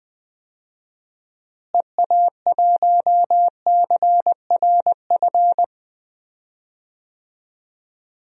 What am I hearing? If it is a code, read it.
Morse code, heard as a keyed tone, about 20 wpm, 700 Hz. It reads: EA1CRF
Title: DXCC-100 Call Signs - 6 Characters - Repeated 2x - 20wpm